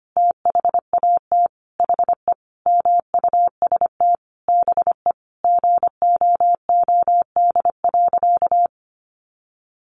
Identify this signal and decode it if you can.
Morse code, heard as a keyed tone, about 25 wpm, 700 Hz. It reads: THAT HE MUST BE GOOD.
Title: Sentences from Top 1500 Words 25wpm